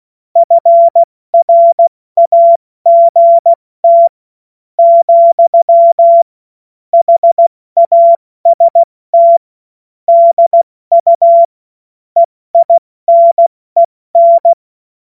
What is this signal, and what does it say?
Morse code, heard as a keyed tone, about 16 wpm, spaced as for 13 wpm, 680 Hz. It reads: FRAGT, HAST DU EINEN
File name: Morsekurs / Witze / Witz 24. August 2021, 70 F80